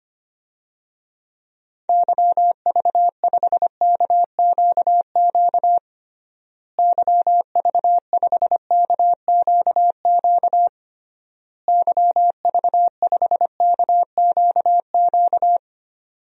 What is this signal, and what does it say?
Morse code, heard as a keyed tone, about 25 wpm, 700 Hz. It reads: YV5KQQ YV5KQQ YV5KQQ
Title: DXCC-100 Call Signs - 6 Characters - Repeated 3x - 25wpm